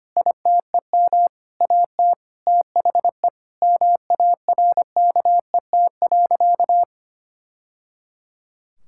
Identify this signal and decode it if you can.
Morse code, heard as a keyed tone, about 25 wpm, 700 Hz. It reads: ITEM AT THE MARKET.
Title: Sentences from Top 1000 Words - Review Part 6 - 25wpm